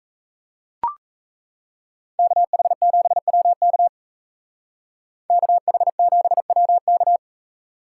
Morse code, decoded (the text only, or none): KH7WK KH7WK